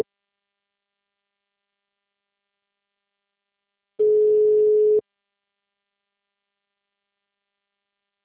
Ringback tone (Europe, 425 Hz).
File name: out- -2030- - - .94876